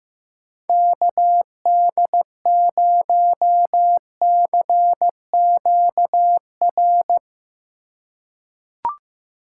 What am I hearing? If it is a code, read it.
Morse code, heard as a keyed tone, about 15 wpm, 700 Hz. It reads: KD0CQR
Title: USA Call Signs - 6 Characters - 8x Character Spacing - 15wpm